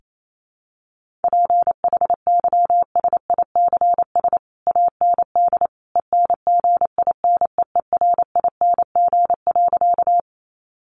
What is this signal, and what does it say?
Morse code, heard as a keyed tone, about 28 wpm, 700 Hz. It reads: PHYSICS AND ENGINEERING.
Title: Sentences from Top 1500 Words 28wpm